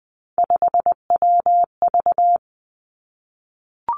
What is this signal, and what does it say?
Morse code, heard as a keyed tone, about 20 wpm, 700 Hz. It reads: 5WV
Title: Three Challenging Letter-Number Combinations 20wpm